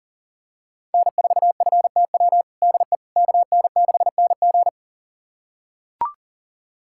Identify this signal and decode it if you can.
Morse code, heard as a keyed tone, about 40 wpm, 700 Hz. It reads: N4FTW DE KN6NG
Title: USA Callsign DE Callsign - 5 Characters - 40wpm